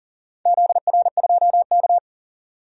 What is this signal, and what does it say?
Morse code, heard as a keyed tone, about 40 wpm, 700 Hz. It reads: ZR2K
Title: DXCC-100 Call Signs - 4 Characters - 40wpm